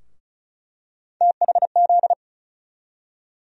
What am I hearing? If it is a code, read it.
Morse code, heard as a keyed tone, about 36 wpm, 700 Hz. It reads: THZ
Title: USA Call Sign Postfixes - 3 Letters 35wpm